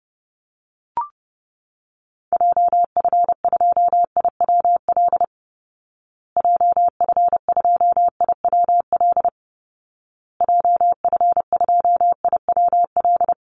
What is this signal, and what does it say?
Morse code, heard as a keyed tone, about 30 wpm, 700 Hz. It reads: JF2IWL JF2IWL JF2IWL
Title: Call Signs Repeated 2x - 30wpm